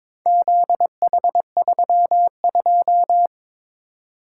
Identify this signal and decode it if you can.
Morse code, heard as a keyed tone, about 22 wpm, 700 Hz. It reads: ZH32
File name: practice / 22wpm / Four Letter-Number Combinations - 22wpm